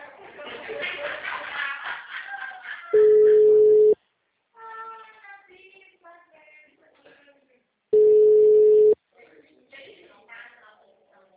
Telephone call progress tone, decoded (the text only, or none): ringback tone